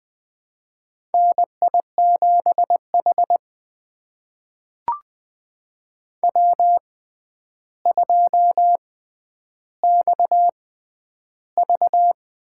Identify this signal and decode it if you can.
Morse code, heard as a keyed tone, about 20 wpm, spaced as for 10 wpm, 700 Hz. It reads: NI7H W2XV
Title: USA Call Signs - 4 Characters - 6x Character Spacing - 20wpm